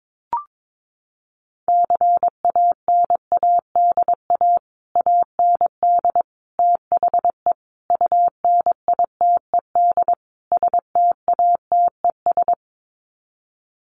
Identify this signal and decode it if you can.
Morse code, heard as a keyed tone, about 22 wpm, 700 Hz. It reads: CANADA AND THE UNITED STATES
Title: Sets of 5 Words Encore 22wpm